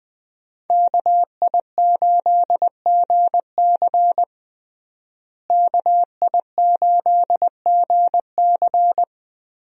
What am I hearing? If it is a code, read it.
Morse code, heard as a keyed tone, about 21 wpm, 700 Hz. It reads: KI8GC KI8GC
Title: USA Call Signs - 5 Characters - Repeated 3x - 20wpm